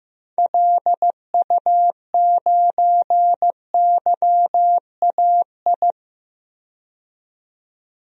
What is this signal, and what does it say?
Morse code, heard as a keyed tone, about 15 wpm, 700 Hz. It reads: LU9YAI
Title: DXCC-100 Call Signs - 6 Characters - 15wpm